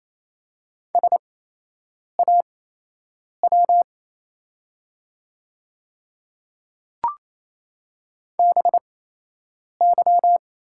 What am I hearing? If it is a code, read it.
Morse code, heard as a keyed tone, about 27 wpm, spaced as for 8 wpm, 690 Hz. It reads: SAW BY
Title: 8x Character Spacing - Top 300 Words - 28wpm